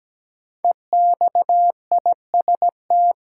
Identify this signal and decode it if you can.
Morse code, heard as a keyed tone, about 17 wpm, 700 Hz. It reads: EXIST
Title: Sentences from Top 1000 Words - Review Part 5 - New Word Drill - 17wpm